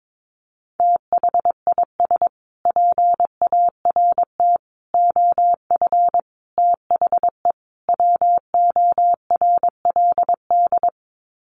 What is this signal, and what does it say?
Morse code, heard as a keyed tone, about 22 wpm, 700 Hz. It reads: THIS PART OF THE WORLD
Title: Sets of 5 Words Encore2 22wpm